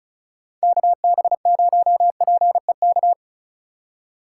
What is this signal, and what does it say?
Morse code, heard as a keyed tone, about 42 wpm, 700 Hz. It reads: KB0PEK
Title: Speed Racing - USA Call Signs 35wpm